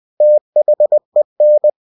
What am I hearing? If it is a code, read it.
Morse code, heard as a keyed tone, about 20 wpm, 600 Hz. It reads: THEN